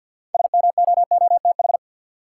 Morse code, heard as a keyed tone, about 49 wpm, 700 Hz. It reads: SMOOTH
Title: Sentences from Top 1400 Words - New Word Drill - 50wpm